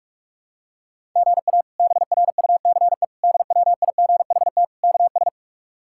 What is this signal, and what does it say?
Morse code, heard as a keyed tone, about 41 wpm, 700 Hz. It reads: GA BRUCE DWIGHT KS
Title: K1USN SST QSO Practice - 45wpm